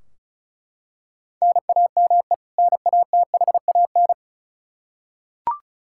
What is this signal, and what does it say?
Morse code, heard as a keyed tone, about 34 wpm, 700 Hz. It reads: NAME NATHAN
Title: Practice QSO - Exchange 1 - Beginning - 35wpm